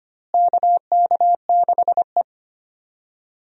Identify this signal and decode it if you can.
Morse code, heard as a keyed tone, about 25 wpm, 700 Hz. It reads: KK6E
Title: USA Call Signs - 4 Characters - Repeated 3x - 25wpm